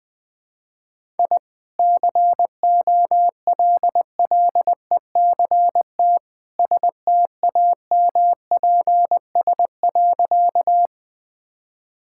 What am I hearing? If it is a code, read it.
Morse code, heard as a keyed tone, about 20 wpm, 700 Hz. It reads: I COLLECT STAMPS.
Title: Easy Sentences with Hints - 20wpm